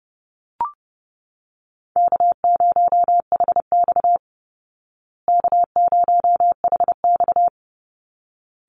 Morse code, heard as a keyed tone, about 30 wpm, 700 Hz. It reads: K0HX K0HX
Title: Easy Call Signs Repeated 2x - 30wpm